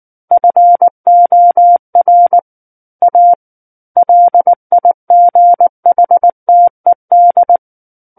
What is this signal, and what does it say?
Morse code, heard as a keyed tone, about 19 wpm, 700 Hz. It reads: FOR A LIGHTED
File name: Book One / 10-Chapter 10 - 19 WPM